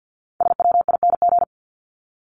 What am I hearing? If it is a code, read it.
Morse code, heard as a keyed tone, about 50 wpm, 700 Hz. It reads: SWING
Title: Top 1751-2000 Words 50wpm